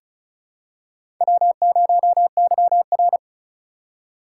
Morse code, heard as a keyed tone, about 36 wpm, 700 Hz. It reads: W0YR
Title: K1USN SST QSO Practice - 35wpm